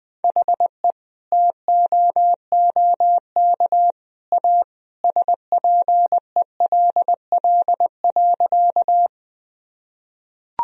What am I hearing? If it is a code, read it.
Morse code, heard as a keyed tone, about 20 wpm, 700 Hz. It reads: HE TOOK A SPELL.
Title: Sentences from Top 1000 Words - Review Part 1 - 20wpm